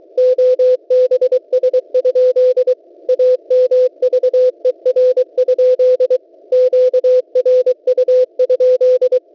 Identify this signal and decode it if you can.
Morse code, heard as a keyed tone, about 23 wpm, 510 Hz. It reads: OBS? AMVER? QRU?
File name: WebSDR records / w7pua-2.ddns.net 12808.50 cw